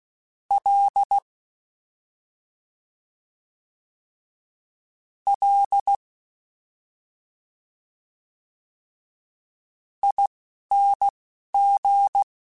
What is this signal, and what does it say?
Morse code, heard as a keyed tone, about 16 wpm, spaced as for 4 wpm, 800 Hz. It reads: LLING
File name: Special INGs-mp3s / Special INGs-mp3s / Slow ing @10